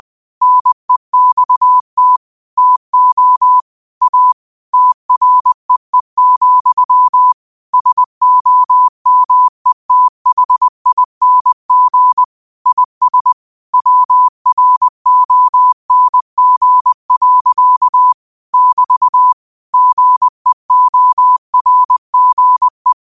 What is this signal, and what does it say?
Morse code, heard as a keyed tone, about 20 wpm, 1000 Hz. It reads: NEXT TO A TREE, SOMETHING IS WRONG. <BT> GEORGE
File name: qotdfiles / 20wpm